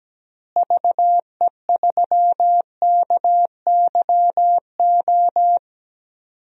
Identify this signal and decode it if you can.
Morse code, heard as a keyed tone, about 17 wpm, 700 Hz. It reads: VE3KYO
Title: DXCC-100 Call Signs - 6 Characters - Repeated 2x - 17wpm